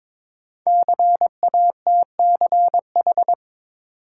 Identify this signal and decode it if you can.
Morse code, heard as a keyed tone, about 22 wpm, 700 Hz. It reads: CATCH
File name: 5 Letter Words - First 3 Letters Given - 22wpm